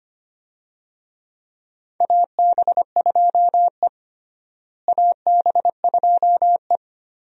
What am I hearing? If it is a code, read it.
Morse code, heard as a keyed tone, about 25 wpm, 700 Hz. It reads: AB2E AB2E
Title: USA Call Signs - 4 Characters - Repeated 2x - 25wpm